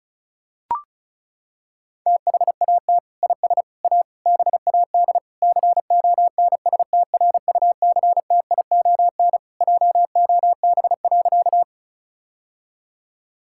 Morse code, heard as a keyed tone, about 36 wpm, 700 Hz. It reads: THAT IS A BAD CONSTRUCTION JOB.
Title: Sentences from Top 1600 Words 35wpm